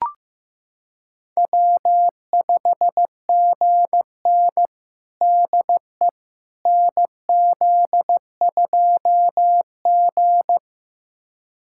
Morse code, heard as a keyed tone, about 15 wpm, 700 Hz. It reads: W5GN DE NZ2G
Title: USA Callsign DE Callsign - 4 Characters - 15wpm